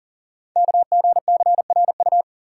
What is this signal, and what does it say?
Morse code, heard as a keyed tone, about 38 wpm, 690 Hz. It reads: KGCRU
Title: Code Groups - 5 Letters - 40wpm